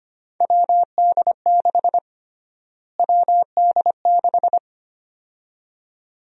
Morse code, heard as a keyed tone, about 25 wpm, 700 Hz. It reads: WD6 WD6
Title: USA Call Sign Prefixes - Repeated 2x - 25wpm